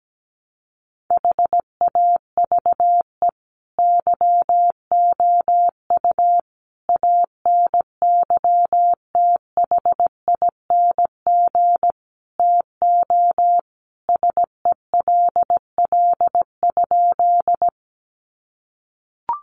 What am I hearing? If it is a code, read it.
Morse code, heard as a keyed tone, about 17 wpm, 700 Hz. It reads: HAVE YOU ANYTHING TO SELL?
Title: Sentences from Top 1100 Words 17wpm